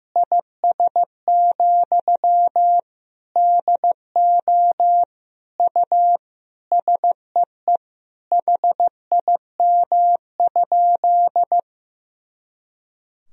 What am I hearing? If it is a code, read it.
Morse code, heard as a keyed tone, about 15 wpm, 700 Hz. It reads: IS, DO U SEE HIM?
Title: Sentences from Top 100 Words and Abbreviations 15wpm